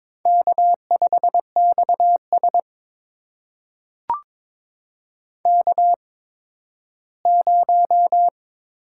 Morse code, heard as a keyed tone, about 22 wpm, spaced as for 13 wpm, 700 Hz. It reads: K5XS K0
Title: USA Call Signs - 4 Characters - 8x Character Spacing - 22wpm